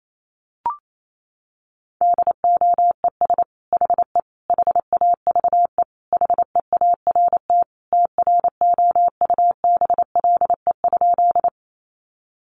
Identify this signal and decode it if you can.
Morse code, heard as a keyed tone, about 28 wpm, 700 Hz. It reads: DOES HE HAVE HEART TROUBLE?
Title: Sentences from Top 1200 Words 28wpm